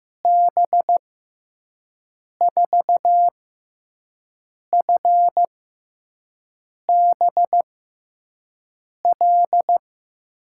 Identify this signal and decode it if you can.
Morse code, heard as a keyed tone, about 15 wpm, spaced as for 5 wpm, 700 Hz. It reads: B4FBL
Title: USA Call Signs - 6 Characters - 6x Character Spacing - 15wpm